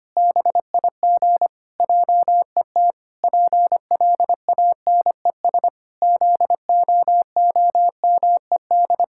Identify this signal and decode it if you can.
Morse code, heard as a keyed tone, about 25 wpm, 700 Hz. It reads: BIG JET PLANES ZOOMED
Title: Pangram Challenge v2 - 25wpm